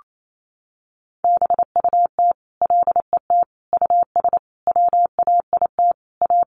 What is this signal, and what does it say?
Morse code, heard as a keyed tone, about 28 wpm, 700 Hz. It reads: BUT LET US WAIT A
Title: Sentences from Top 1500 Words 28wpm